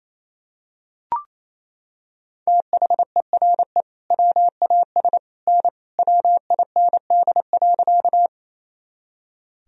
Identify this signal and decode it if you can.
Morse code, heard as a keyed tone, about 28 wpm, 700 Hz. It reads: THERE WAS N WIND.
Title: Sentences from Top 400 Words and Abbreviations 28wpm